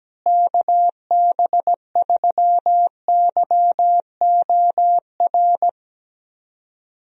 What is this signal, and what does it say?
Morse code, heard as a keyed tone, about 17 wpm, 700 Hz. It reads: KB3YOR
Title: USA Call Signs - 6 Characters - Repeated 2x - 17wpm